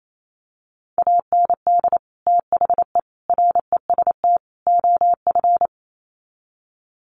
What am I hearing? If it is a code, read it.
Morse code, heard as a keyed tone, about 28 wpm, 700 Hz. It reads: AND THE REST OF
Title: Sets of 4 Words 28wpm